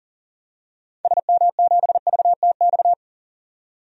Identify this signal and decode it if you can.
Morse code, heard as a keyed tone, about 40 wpm, 700 Hz. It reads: SM7VTX
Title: DXCC-100 Call Signs - 6 Characters - 40wpm